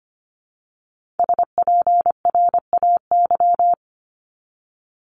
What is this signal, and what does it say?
Morse code, heard as a keyed tone, about 25 wpm, 700 Hz. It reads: SPRAY
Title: Multi-letter Phonemes Repeated 3x 25wpm